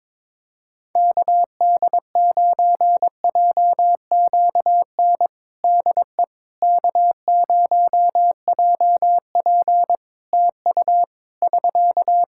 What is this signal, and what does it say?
Morse code, heard as a keyed tone, about 22 wpm, 700 Hz. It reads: KD9JQN DE K0JP TU <SK>
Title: Practice QSO - Exchange 4 - Ending - 22wpm